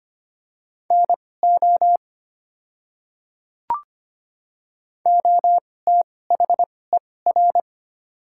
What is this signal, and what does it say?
Morse code, heard as a keyed tone, about 25 wpm, spaced as for 18 wpm, 700 Hz. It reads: NO OTHER
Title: 2x Character Spacing - Top 100 Words - 25wpm